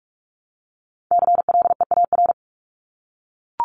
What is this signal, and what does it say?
Morse code, heard as a keyed tone, about 44 wpm, 700 Hz. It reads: CLEAR
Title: Top 500 Words 45wpm